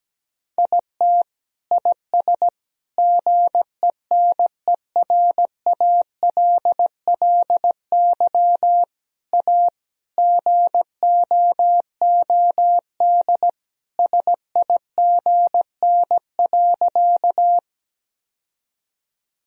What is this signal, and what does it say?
Morse code, heard as a keyed tone, about 17 wpm, 700 Hz. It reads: IT IS GENERALLY A GOOD SIGN.
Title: Sentences from Top 1000 Words - Review Part 4 - 17wpm